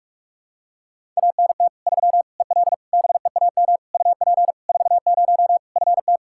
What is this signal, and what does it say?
Morse code, heard as a keyed tone, about 46 wpm, 700 Hz. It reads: ANT 3 EL BEAM UP 40 FT
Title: K1USN SST QSO Practice 45wpm